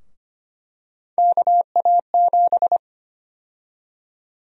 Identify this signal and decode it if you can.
Morse code, heard as a keyed tone, about 25 wpm, 700 Hz. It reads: KA7
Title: USA Call Sign Prefixes - 25wpm